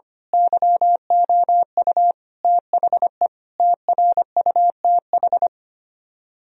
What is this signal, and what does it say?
Morse code, heard as a keyed tone, about 24 wpm, 700 Hz. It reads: YOU THE TRUTH
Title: Speed-Racing - Sets of 5 Words 25wpm